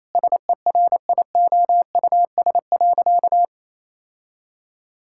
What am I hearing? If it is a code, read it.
Morse code, heard as a keyed tone, about 28 wpm, 700 Hz. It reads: SERIOUS.
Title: Sentences from Top 1600 Words 28wpm